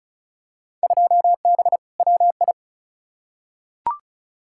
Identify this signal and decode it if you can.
Morse code, heard as a keyed tone, about 36 wpm, 700 Hz. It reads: 2B WI
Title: Field Day Exchanges - 35wpm